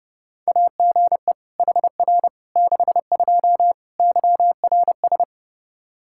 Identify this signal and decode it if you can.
Morse code, heard as a keyed tone, about 30 wpm, 700 Hz. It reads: AGE HR 62 YRS
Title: Practice QSO - Exchange 3 - More Info - 30wpm